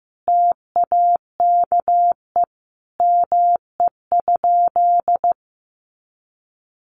Morse code, heard as a keyed tone, about 15 wpm, 700 Hz. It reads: TAKE ME?
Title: Sentences from Top 200 Words 15wpm